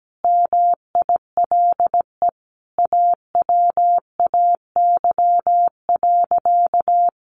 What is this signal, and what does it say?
Morse code, heard as a keyed tone, about 17 wpm, 700 Hz. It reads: MILE AWAY.
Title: Sentences from Top 1300 Words 17wpm